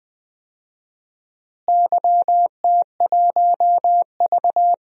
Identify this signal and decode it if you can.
Morse code, heard as a keyed tone, about 20 wpm, 700 Hz. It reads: YT1V
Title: DXCC-100 Call Signs - 4 Characters - Repeated 2x - 20wpm